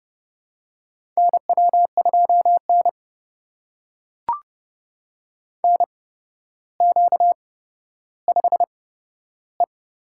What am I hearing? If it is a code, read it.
Morse code, heard as a keyed tone, about 31 wpm, spaced as for 13 wpm, 700 Hz. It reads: NW2N NQ5E
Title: USA Call Signs - 4 Characters - 8x Character Spacing - 30wpm